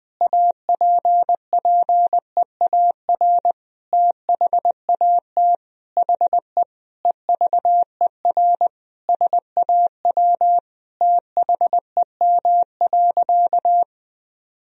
Morse code, heard as a keyed tone, about 20 wpm, 700 Hz. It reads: APPEAR THAT HE EVER SAW THEM.
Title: Sentences from Top 1000 Words - Review Part 2 - 20wpm